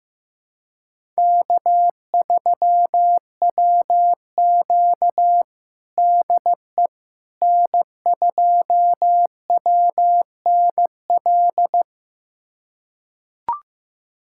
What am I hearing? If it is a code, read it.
Morse code, heard as a keyed tone, about 15 wpm, 700 Hz. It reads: K3WQ DE N2WNL
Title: USA Callsign DE Callsign 15wpm